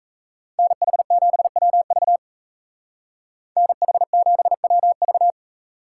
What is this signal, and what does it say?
Morse code, heard as a keyed tone, about 33 wpm, 700 Hz. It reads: NH7WV NH7WV
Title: Speed Racing - USA 2x2 Call Signs 28wpm